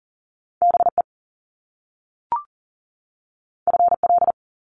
Morse code, heard as a keyed tone, about 42 wpm, 700 Hz. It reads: BE FL
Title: Top Two Letter Combinations in Words 40wpm